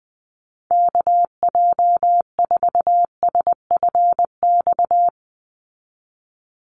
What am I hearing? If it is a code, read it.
Morse code, heard as a keyed tone, about 20 wpm, 700 Hz. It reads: KJ4SFX
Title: USA Call Signs 20wpm